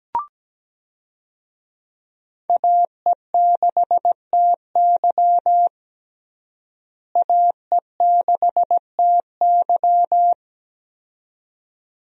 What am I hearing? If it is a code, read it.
Morse code, heard as a keyed tone, about 17 wpm, 700 Hz. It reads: AE6TY AE6TY
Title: USA Call Signs - 5 Characters - Repeated 2x - 17wpm